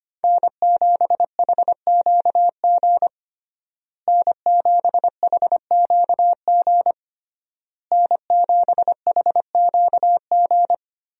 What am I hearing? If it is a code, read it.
Morse code, heard as a keyed tone, about 25 wpm, 700 Hz. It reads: N7HQG N7HQG N7HQG
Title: USA Call Signs - 5 Characters - Repeated 3x - 25wpm